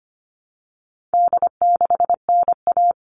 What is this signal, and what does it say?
Morse code, heard as a keyed tone, about 25 wpm, 700 Hz. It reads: D6NA
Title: All Cut Numbers - 4-Digit 25wpm